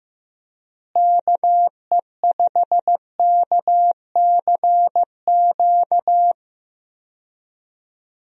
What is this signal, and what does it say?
Morse code, heard as a keyed tone, about 15 wpm, 700 Hz. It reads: KE5KCQ
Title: USA Call Signs - 6 Characters - Repeated 2x - 15wpm